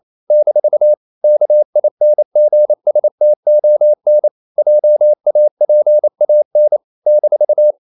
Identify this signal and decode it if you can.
Morse code, heard as a keyed tone, about 28 wpm, 600 Hz. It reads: - KINGSTON JAPAN -
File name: CW / Practice / NationalCapitals28